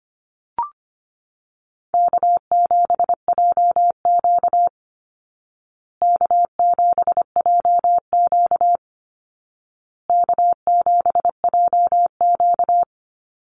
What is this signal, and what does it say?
Morse code, heard as a keyed tone, about 25 wpm, 700 Hz. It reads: K7JQ K7JQ K7JQ
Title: Easy Call Signs Repeated 3x - 25wpm